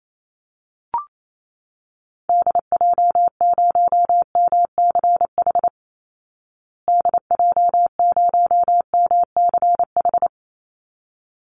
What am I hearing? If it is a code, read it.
Morse code, heard as a keyed tone, about 28 wpm, 700 Hz. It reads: DJ0MCH DJ0MCH
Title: Call Signs Repeated 3x - 28wpm